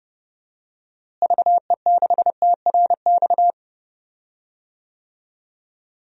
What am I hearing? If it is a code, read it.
Morse code, heard as a keyed tone, about 30 wpm, 700 Hz. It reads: VE6TRX
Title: DXCC-100 Call Signs - 6 Characters - Repeated 2x - 30wpm